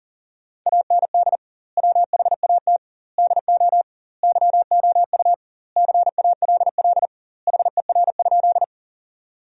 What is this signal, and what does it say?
Morse code, heard as a keyed tone, about 40 wpm, 700 Hz. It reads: AND WHAT DO YOU CALL HER?
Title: 2x Word-Spacing - Sentences from Top 100 Words - 40wpm